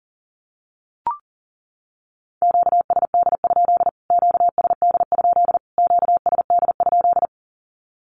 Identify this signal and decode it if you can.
Morse code, heard as a keyed tone, about 40 wpm, 700 Hz. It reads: QSD? QSD? QSD?
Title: Q-Codes Repeated 3x 40wpm